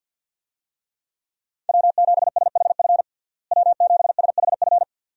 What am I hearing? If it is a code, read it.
Morse code, heard as a keyed tone, about 50 wpm, 700 Hz. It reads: W7SHF W7SHF
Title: USA Call Signs - 5 Characters - Repeated 2x - 50wpm